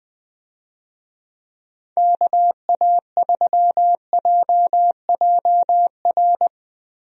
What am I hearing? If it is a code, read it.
Morse code, heard as a keyed tone, about 20 wpm, 700 Hz. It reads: KA3JJR
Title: USA Call Signs - 6 Characters - 20wpm